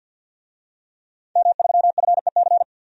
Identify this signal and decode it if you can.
Morse code, heard as a keyed tone, about 52 wpm, 700 Hz. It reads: M3FEC
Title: DXCC-100 Call Signs - 5 Characters - Repeated 2x - 50wpm